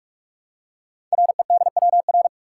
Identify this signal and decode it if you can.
Morse code, heard as a keyed tone, about 44 wpm, 700 Hz. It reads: REDWR
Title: Code Groups - 5 Letters - 45wpm